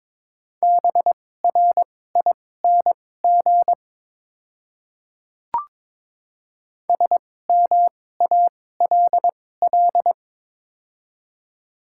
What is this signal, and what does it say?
Morse code, heard as a keyed tone, about 21 wpm, spaced as for 16 wpm, 700 Hz. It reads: BRING SMALL
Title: 2x Character Spacing - Top 500 Words - 22wpm